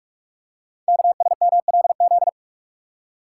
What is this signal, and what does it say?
Morse code, heard as a keyed tone, about 44 wpm, 700 Hz. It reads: KSMLZ
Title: Code Groups - 5 Letters - 45wpm